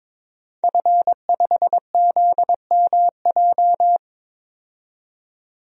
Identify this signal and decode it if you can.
Morse code, heard as a keyed tone, about 21 wpm, 700 Hz. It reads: F5ZMJ
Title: DXCC-100 Call Signs - 5 Characters - 22wpm